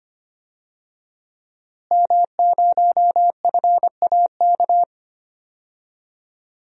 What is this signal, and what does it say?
Morse code, heard as a keyed tone, about 25 wpm, 700 Hz. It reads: M0FAK
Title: DXCC-100 Call Signs - 5 Characters - Repeated 3x - 25wpm